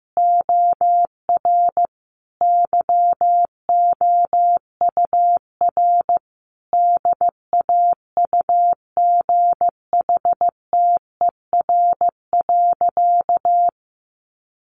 Morse code, heard as a keyed tone, about 15 wpm, 700 Hz. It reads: OR YOUR DAUGHTER.
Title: Sentences from Top 1500 Words 15wpm